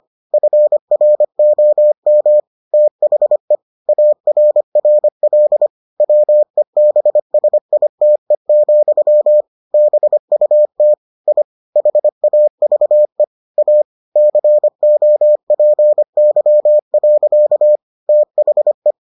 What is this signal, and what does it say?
Morse code, heard as a keyed tone, about 25 wpm, 600 Hz. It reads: FROM THE ARRL WEBSITE, BUT I HAVE A COPY. THE